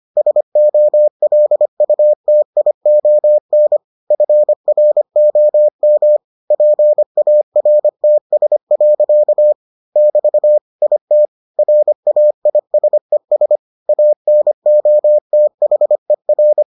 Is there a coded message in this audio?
Yes: Morse code from 0.2 to 16.6 s, heard as a keyed tone, about 25 wpm, 600 Hz. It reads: SOLUTION FROM PARTS. <BT> IT RAISES ANOTHER